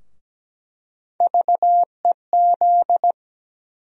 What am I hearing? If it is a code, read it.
Morse code, heard as a keyed tone, about 17 wpm, 700 Hz. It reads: VEZ
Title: USA Call Sign Postfixes - 3 Letters 17wpm